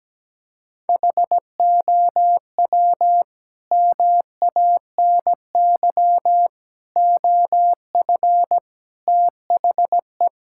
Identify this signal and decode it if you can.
Morse code, heard as a keyed tone, about 17 wpm, 700 Hz. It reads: HOW MANY OF THE
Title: Sentences from Top 1000 Words - Review Part 6 - 17wpm